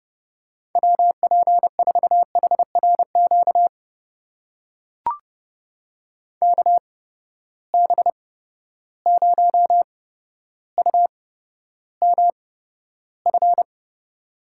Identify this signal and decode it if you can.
Morse code, heard as a keyed tone, about 30 wpm, spaced as for 13 wpm, 700 Hz. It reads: WP4HRQ KB0UMF
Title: USA Call Signs - 6 Characters - 8x Character Spacing - 30wpm